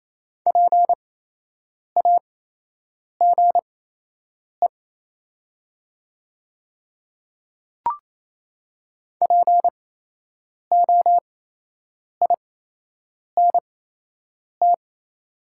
Morse code, heard as a keyed tone, about 28 wpm, spaced as for 8 wpm, 700 Hz. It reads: PAGE POINT
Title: 8x Character Spacing - Top 300 Words - 28wpm